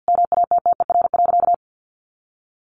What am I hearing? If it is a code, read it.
Morse code, heard as a keyed tone, about 50 wpm, 700 Hz. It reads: MATTER.
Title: Sentences from Top 1200 Words 50wpm